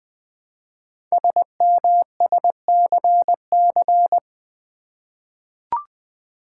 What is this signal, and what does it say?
Morse code, heard as a keyed tone, about 20 wpm, 700 Hz. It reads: SMSCC
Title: Code Groups - 5 Letters - 20wpm